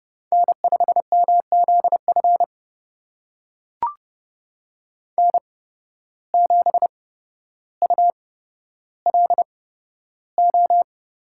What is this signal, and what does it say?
Morse code, heard as a keyed tone, about 30 wpm, spaced as for 13 wpm, 700 Hz. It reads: N5MZF N7ULO
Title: USA Call Signs - 5 Characters - 8x Character Spacing - 30wpm